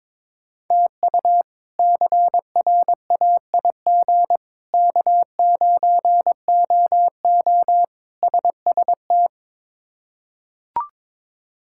Morse code, heard as a keyed tone, about 22 wpm, 700 Hz. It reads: TU CRAIG K9OO SST
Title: K1USN SST QSO Practice - 22wpm